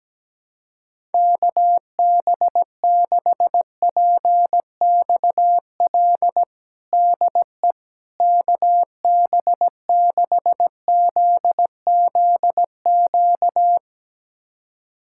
Morse code, heard as a keyed tone, about 17 wpm, 700 Hz. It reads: KB6PXL DE KB6ZZQ
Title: USA Callsign DE Callsign - 6 Characters - 17wpm